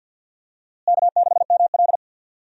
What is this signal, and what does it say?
Morse code, heard as a keyed tone, about 52 wpm, 700 Hz. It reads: K6ML
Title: USA Call Signs - 4 Characters - Repeated 2x - 50wpm